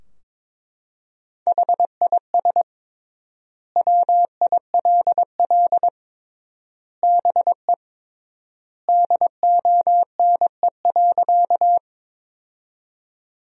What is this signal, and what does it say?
Morse code, heard as a keyed tone, about 22 wpm, 700 Hz. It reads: HIS WILL BE DONE.
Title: 3x Word-Spacing - Sentences from Top 500 Words - 22wpm